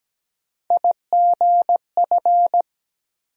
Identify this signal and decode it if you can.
Morse code, heard as a keyed tone, about 17 wpm, 700 Hz. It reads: IGF
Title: USA Call Sign Postfixes - 3 Letters 17wpm